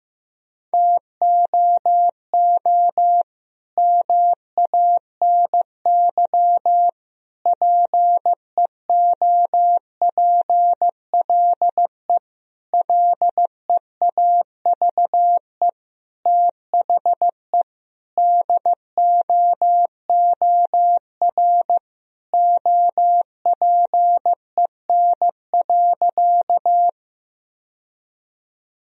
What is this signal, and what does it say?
Morse code, heard as a keyed tone, about 16 wpm, 700 Hz. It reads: TOO MANY PEOPLE LEAVE THE DOOR OPEN.
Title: Sentences from Top 1000 Words - Review Part 1 - 15wpm